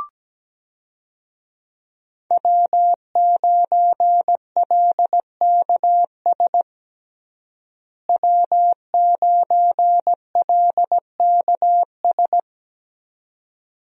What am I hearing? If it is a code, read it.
Morse code, heard as a keyed tone, about 17 wpm, 700 Hz. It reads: W9LKS W9LKS
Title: USA Call Signs - 5 Characters - Repeated 2x - 17wpm